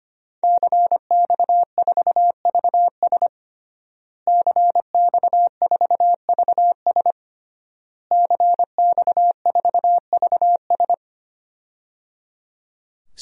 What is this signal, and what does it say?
Morse code, heard as a keyed tone, about 25 wpm, 700 Hz. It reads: CX4VS CX4VS CX4VS
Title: DXCC-100 Call Signs - 5 Characters - Repeated 3x - 25wpm